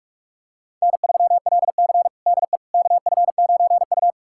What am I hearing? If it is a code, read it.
Morse code, heard as a keyed tone, about 46 wpm, 700 Hz. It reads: N3LC DE KF9U
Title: USA Callsign DE Callsign - 4 Characters - 45wpm